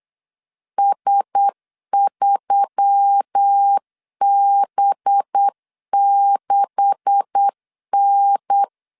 Morse code, heard as a keyed tone, about 8 wpm, 800 Hz. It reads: S3B6N
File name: CW / DET / 1 9 1500C S B 8wpm 5624